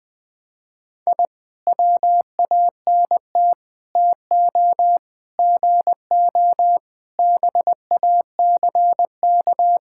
Morse code, heard as a keyed tone, about 21 wpm, 700 Hz. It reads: I WANT TO GO BACK
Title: Sentences with 4-Word Phrases - 20wpm